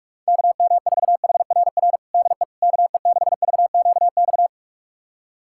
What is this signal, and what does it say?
Morse code, heard as a keyed tone, about 41 wpm, 700 Hz. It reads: KM4HRR DE KE6VQX
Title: USA Callsign DE Callsign 45wpm